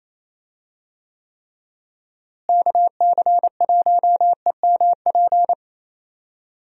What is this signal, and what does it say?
Morse code, heard as a keyed tone, about 28 wpm, 700 Hz. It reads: KC1EMP
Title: USA Call Signs - 6 Characters - Repeated 2x - 28wpm